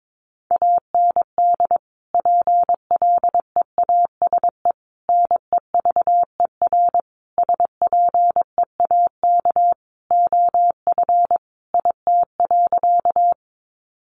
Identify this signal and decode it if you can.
Morse code, heard as a keyed tone, about 22 wpm, 700 Hz. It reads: AND PLEASE NEVER SPEAK OF IT.
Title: Sentences from Top 1500 Words 22wpm